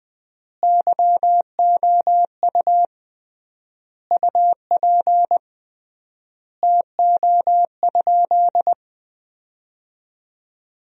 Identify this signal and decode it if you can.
Morse code, heard as a keyed tone, about 20 wpm, 700 Hz. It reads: YOU UP TO?
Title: 3x Word-Spacing - Sentences from Top 100 Words - 20wpm